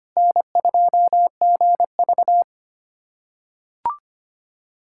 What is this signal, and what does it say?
Morse code, heard as a keyed tone, about 25 wpm, 700 Hz. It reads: N2GV
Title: USA Callsign DE Callsign - 4 Characters - 25wpm